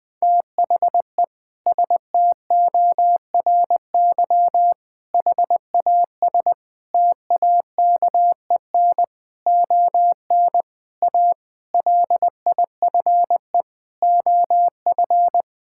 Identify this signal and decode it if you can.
Morse code, heard as a keyed tone, about 20 wpm, 700 Hz. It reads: THE STORY HAS TAKEN ON A LIFE OF
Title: Sentences with 8-Word Phrases - 20wpm